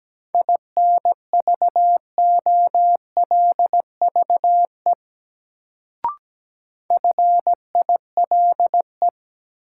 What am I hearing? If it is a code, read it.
Morse code, heard as a keyed tone, about 17 wpm, 700 Hz. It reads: INVOLVE FILE
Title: Sentences from Top 1200 Words - New Word Drill Alt - 17wpm